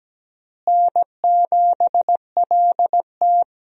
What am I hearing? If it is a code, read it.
Morse code, heard as a keyed tone, about 17 wpm, 700 Hz. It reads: N7LT
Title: USA Call Signs - 4 Characters - Repeated 2x - 17wpm